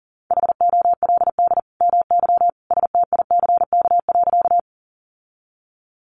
Morse code, heard as a keyed tone, about 40 wpm, 700 Hz. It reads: HOLD MY STICK.
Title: Sentences from Top 1200 Words 40wpm